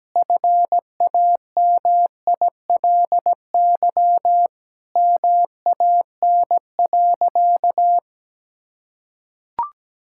Morse code, heard as a keyed tone, about 17 wpm, 700 Hz. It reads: FAMILY MAN.
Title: Sentences from Top 1000 Words - Review Part 5 - 17wpm